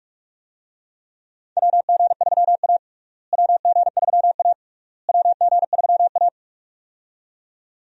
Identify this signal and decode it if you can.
Morse code, heard as a keyed tone, about 43 wpm, 700 Hz. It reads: WG3A WG3A WG3A
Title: USA Call Signs - 4 Characters - Repeated 3x - 45wpm